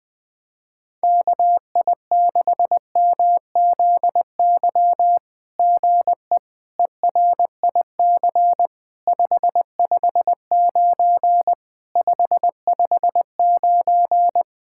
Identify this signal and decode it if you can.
Morse code, heard as a keyed tone, about 20 wpm, 700 Hz. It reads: KI6MZY GE ERIC 559 559
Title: SOTA QSO Practice 20wpm